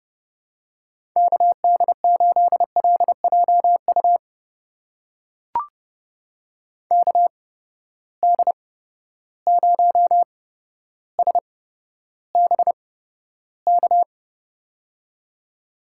Morse code, heard as a keyed tone, about 30 wpm, spaced as for 13 wpm, 700 Hz. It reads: KD8LJU KD0SBK
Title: USA Call Signs - 6 Characters - 8x Character Spacing - 30wpm